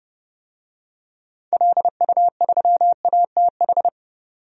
Morse code, heard as a keyed tone, about 30 wpm, 700 Hz. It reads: LU3ATH
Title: DXCC-100 Call Signs - 6 Characters - 30wpm